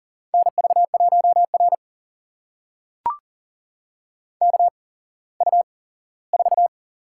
Morse code, heard as a keyed tone, about 39 wpm, spaced as for 19 wpm, 690 Hz. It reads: NV1R KU4
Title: USA Call Signs - 4 Characters - 8x Character Spacing - 40wpm